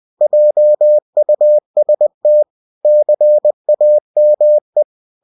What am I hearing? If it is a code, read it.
Morse code, heard as a keyed tone, about 20 wpm, 600 Hz. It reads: JUST CAME
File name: cw-books / winnie-the-pooh / winnie-20wpm-page-0032